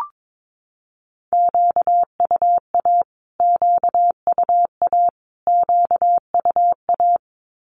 Morse code, heard as a keyed tone, about 22 wpm, 700 Hz. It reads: QUA QUA QUA
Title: Q-Codes Repeated 3x 22wpm